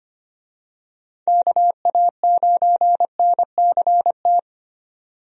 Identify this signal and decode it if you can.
Morse code, heard as a keyed tone, about 25 wpm, 700 Hz. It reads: KA9NCT
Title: USA Call Signs - 6 Characters - 25wpm